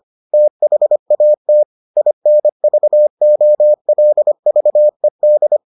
Morse code, heard as a keyed tone, about 25 wpm, 600 Hz. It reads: THAT INVOLVED